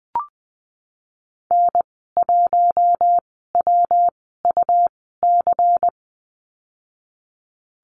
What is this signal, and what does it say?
Morse code, heard as a keyed tone, about 20 wpm, spaced as for 15 wpm, 700 Hz. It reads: N1WUC
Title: 161 Lesson 26g - Letter g - Call Signs 20wpm